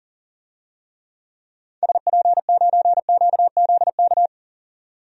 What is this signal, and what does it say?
Morse code, heard as a keyed tone, about 40 wpm, 700 Hz. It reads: SP9QZK
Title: DXCC-100 Call Signs - 6 Characters - 40wpm